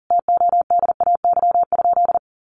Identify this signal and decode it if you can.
Morse code, heard as a keyed tone, about 40 wpm, 700 Hz. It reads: TODAY?
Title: Sentences from Top 1500 Words 40wpm